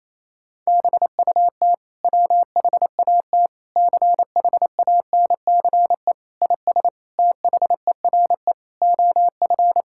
Morse code, heard as a keyed tone, about 28 wpm, 690 Hz. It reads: BUT WHAT CHANCE IS THERE OF
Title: Sentences from Top 1000 Words - Review Part 6 - 28wpm